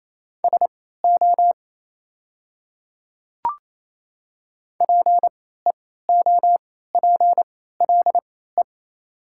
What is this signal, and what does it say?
Morse code, heard as a keyed tone, about 28 wpm, spaced as for 16 wpm, 700 Hz. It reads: SO PEOPLE
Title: 3x Character Spacing - Top 100 Words - 28wpm